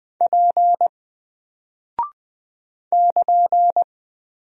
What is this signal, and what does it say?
Morse code, heard as a keyed tone, about 20 wpm, 700 Hz. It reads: P <KN>
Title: Single Character (Letter:Number:Punctuation:Prosign) Alt 20wpm